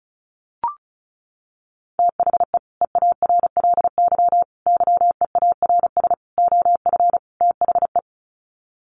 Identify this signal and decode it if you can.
Morse code, heard as a keyed tone, about 34 wpm, 700 Hz. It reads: THE EARLY YEARS OF THE
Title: Sets of 5 Words Encore 35wpm